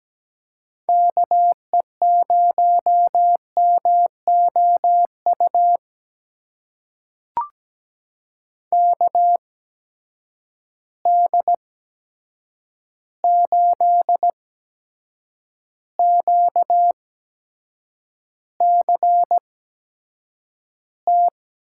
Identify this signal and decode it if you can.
Morse code, heard as a keyed tone, about 17 wpm, spaced as for 7 wpm, 700 Hz. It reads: KE0MOU KD8QCT
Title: USA Call Signs - 6 Characters - 8x Character Spacing - 17wpm